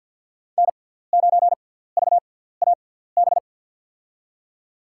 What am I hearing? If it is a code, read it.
Morse code, heard as a keyed tone, about 53 wpm, spaced as for 17 wpm, 700 Hz. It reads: N9VAB
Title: USA Call Signs - 6x Character Spacing - 50wpm